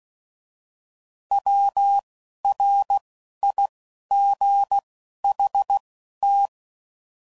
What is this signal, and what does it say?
Morse code, heard as a keyed tone, about 16 wpm, 800 Hz. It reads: W R I G H T